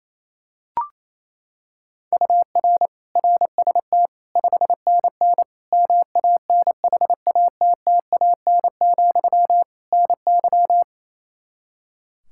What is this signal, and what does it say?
Morse code, heard as a keyed tone, about 28 wpm, 700 Hz. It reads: UR RST 5NN MANHATTAN, NY
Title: Practice QSO - Exchange 1 - Beginning - 28wpm